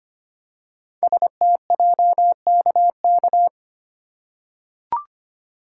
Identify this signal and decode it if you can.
Morse code, heard as a keyed tone, about 25 wpm, 700 Hz. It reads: STJKK
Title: Code Groups - 5 Letters - 25wpm